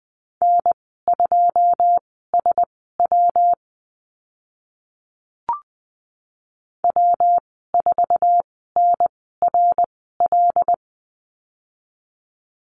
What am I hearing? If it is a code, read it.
Morse code, heard as a keyed tone, about 20 wpm, spaced as for 14 wpm, 700 Hz. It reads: N2SW W4NRL
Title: 140 Lesson 23g - Letter f - Call Signs 20wpm